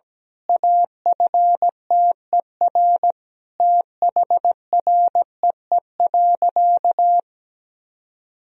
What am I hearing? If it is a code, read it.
Morse code, heard as a keyed tone, about 17 wpm, 700 Hz. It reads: AFTER THREE.
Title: Sentences from Top 1000 Words - Review Part 1 - 17wpm